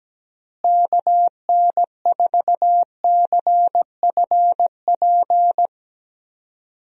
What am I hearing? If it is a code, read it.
Morse code, heard as a keyed tone, about 17 wpm, 700 Hz. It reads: KN4CFP
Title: USA Call Signs - 6 Characters - Repeated 3x - 17wpm